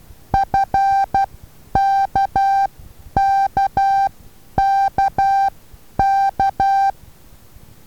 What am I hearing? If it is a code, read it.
Morse code, heard as a keyed tone, about 12 wpm, spaced as for 9 wpm, 780 Hz. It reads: FKKKK
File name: Taviro HA6VA B